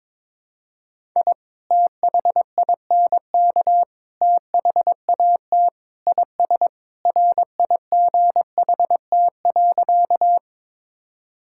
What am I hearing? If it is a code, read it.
Morse code, heard as a keyed tone, about 22 wpm, 700 Hz. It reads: I THINK THAT IS RIGHT.
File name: practice / 22wpm / Sentences with 2-Word Phrases - 22wpm